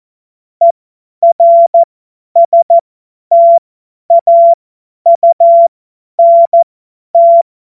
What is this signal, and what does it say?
Morse code, heard as a keyed tone, about 14 wpm, spaced as for 10 wpm, 680 Hz. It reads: ERSTAUNT
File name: Morsekurs / Witze / Witz 19. Oktober 2021, 50 F70